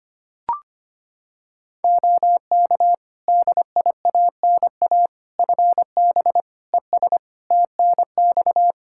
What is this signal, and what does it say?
Morse code, heard as a keyed tone, about 25 wpm, 700 Hz. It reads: OK DIANA FB ES TNX
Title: Practice QSO - Exchange 2 - Info - 25wpm